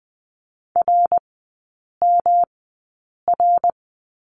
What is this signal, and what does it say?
Morse code, heard as a keyed tone, about 20 wpm, spaced as for 8 wpm, 690 Hz. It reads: RMR
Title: 166 Lesson 27e - Number 7 - Three Characters 20wpm